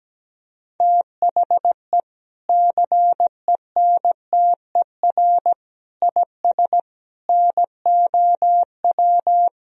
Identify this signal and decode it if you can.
Morse code, heard as a keyed tone, about 17 wpm, 700 Hz. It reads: THE CENTER IS NOW
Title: Sentences from Top 1000 Words - Review Part 2 - 17wpm